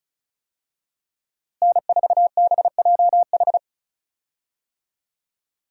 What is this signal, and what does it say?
Morse code, heard as a keyed tone, about 36 wpm, 700 Hz. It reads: N4BJH
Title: USA Call Signs - 5 Characters - Repeated 3x - 35wpm